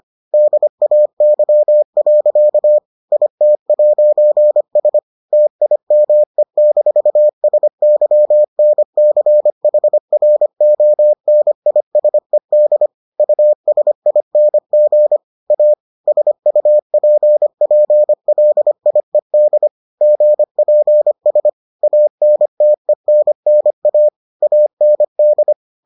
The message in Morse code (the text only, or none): DAY. IT'S TIME-SYNCHRONISED USING A SUPPLIED GPS ANTENNA AND